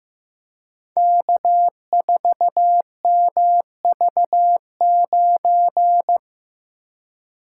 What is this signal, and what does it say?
Morse code, heard as a keyed tone, about 15 wpm, 700 Hz. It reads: K4MV9
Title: Code Groups - 5 Letter:Numbers - 15wpm